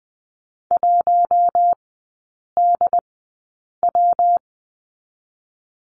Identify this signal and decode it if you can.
Morse code, heard as a keyed tone, about 20 wpm, 700 Hz. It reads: 1 D W
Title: 127 Lesson 21e - Number 6 - Three Characters 20wpm